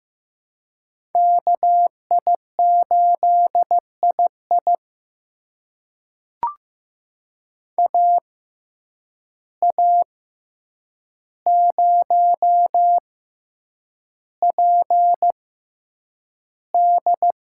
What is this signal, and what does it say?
Morse code, heard as a keyed tone, about 15 wpm, spaced as for 8 wpm, 700 Hz. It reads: KI8II AA0PD
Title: USA Call Signs - 5 Characters - 6x Character Spacing - 15wpm